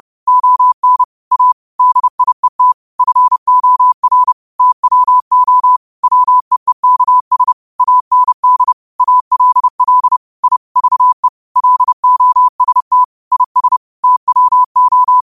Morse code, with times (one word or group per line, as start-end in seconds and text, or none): 0.3-1.0 ON
1.3-1.5 A
1.8-2.7 DIET
3.0-4.3 FOR
4.6-5.8 TWO
6.0-7.5 WEEKS
7.8-8.7 AND
9.0-10.2 ALL
10.4-10.6 I
10.8-11.3 VE
11.6-13.0 LOST
13.3-13.8 IS
14.0-15.2 TWO